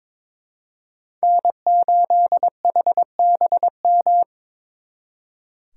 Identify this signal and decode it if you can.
Morse code, heard as a keyed tone, about 26 wpm, 700 Hz. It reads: N8HBM
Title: Speed Racing - USA 1x3 Call Signs 22wpm